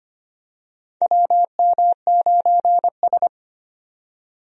Morse code, heard as a keyed tone, about 25 wpm, 700 Hz. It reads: WM9S
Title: USA Call Signs - 4 Characters - 25wpm